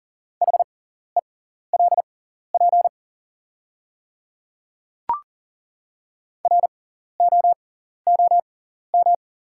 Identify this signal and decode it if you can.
Morse code, heard as a keyed tone, about 38 wpm, spaced as for 14 wpm, 690 Hz. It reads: HELP ROOM
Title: 6x Character Spacing - Top 500 Words - 40wpm